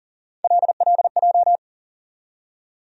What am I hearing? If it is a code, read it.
Morse code, heard as a keyed tone, about 40 wpm, 700 Hz. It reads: LLJ
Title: USA Call Sign Postfixes - 3 Letters 40wpm